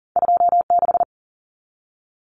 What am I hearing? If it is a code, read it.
Morse code, heard as a keyed tone, about 40 wpm, 700 Hz. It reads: 26
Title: Common Cut Numbers - 2-Digit 40wpm